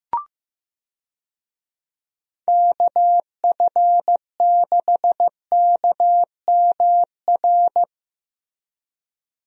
Morse code, heard as a keyed tone, about 15 wpm, 700 Hz. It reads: KF6KMR
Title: USA Call Signs - 6 Characters - 15wpm